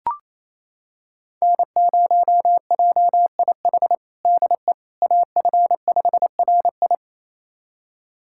Morse code, heard as a keyed tone, about 27 wpm, 700 Hz. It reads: N0JIH DE AF5RI
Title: USA Callsign DE Callsign - 5 Characters - 28wpm